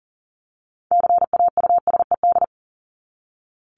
Morse code, heard as a keyed tone, about 40 wpm, 700 Hz. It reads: CAUSED
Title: ED Suffix - Top 100 Words with Base Words 40wpm